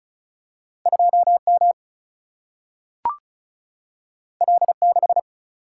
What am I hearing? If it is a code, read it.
Morse code, heard as a keyed tone, about 36 wpm, 700 Hz. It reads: 2M L6
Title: Single-Letter Single-Number 35wpm